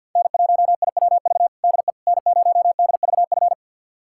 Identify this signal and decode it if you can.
Morse code, heard as a keyed tone, about 50 wpm, 700 Hz. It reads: N1IWV DE N0DVF
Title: USA Callsign DE Callsign - 5 Characters - 50wpm